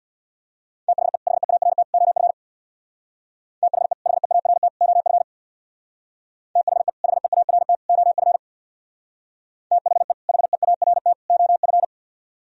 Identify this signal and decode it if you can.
Morse code, heard as a keyed tone, about 63 wpm, 700 Hz. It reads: THE HEART OF THE HEART OF THE HEART OF THE HEART OF